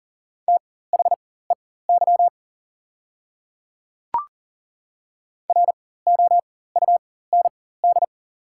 Morse code, heard as a keyed tone, about 40 wpm, spaced as for 19 wpm, 690 Hz. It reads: THEY ROUND
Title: 4x Character Spacing - Top 200 Words - 40wpm